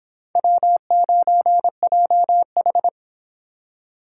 Morse code, heard as a keyed tone, about 25 wpm, 700 Hz. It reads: W9JH